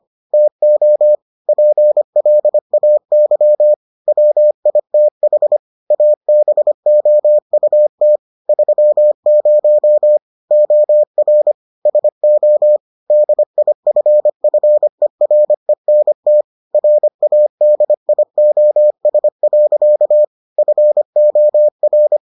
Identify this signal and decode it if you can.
Morse code, heard as a keyed tone, about 25 wpm, 600 Hz. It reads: TO PLAY WITH ABOUT 30 OR SO DIFFERENT RADIOS. FOR